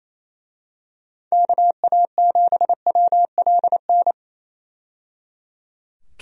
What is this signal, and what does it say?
Morse code, heard as a keyed tone, about 28 wpm, 700 Hz. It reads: KA7WLN
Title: USA Call Signs - 6 Characters - 28wpm